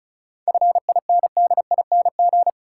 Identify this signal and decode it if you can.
Morse code, heard as a keyed tone, about 34 wpm, 700 Hz. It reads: FINDING
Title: Sentences from Top 1100 Words - New Word Drill Alt - 35wpm